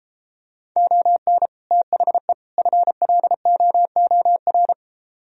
Morse code, heard as a keyed tone, about 34 wpm, 700 Hz. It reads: ON THE FLOOR